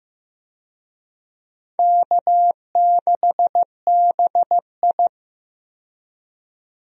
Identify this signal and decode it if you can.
Morse code, heard as a keyed tone, about 15 wpm, 700 Hz. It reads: K6BI
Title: USA Call Signs - 4 Characters - 15wpm